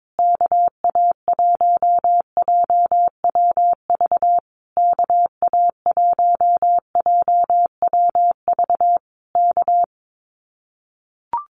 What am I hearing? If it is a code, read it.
Morse code, heard as a keyed tone, about 22 wpm, 700 Hz. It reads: KA1JWV KA1JWV K
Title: Common QSO Phrases Encore 22wpm